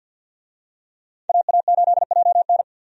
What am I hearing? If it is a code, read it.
Morse code, heard as a keyed tone, about 51 wpm, 700 Hz. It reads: AA8JN
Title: USA Call Signs - 5 Characters - 50wpm